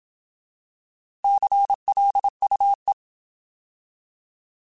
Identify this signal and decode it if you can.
Morse code, heard as a keyed tone, about 27 wpm, 800 Hz. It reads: CLUE
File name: Alphabetical